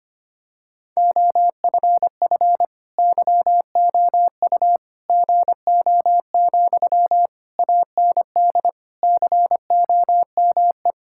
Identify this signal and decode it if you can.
Morse code, heard as a keyed tone, about 25 wpm, 700 Hz. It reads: OFF YOU GO, AND COME
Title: Sentences from Top 1000 Words - Review Part 6 - 25wpm